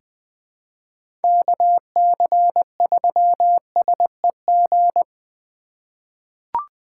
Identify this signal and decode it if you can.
Morse code, heard as a keyed tone, about 20 wpm, 700 Hz. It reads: KC3SEG
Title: 2023 POTA Activator Call Signs - 20wpm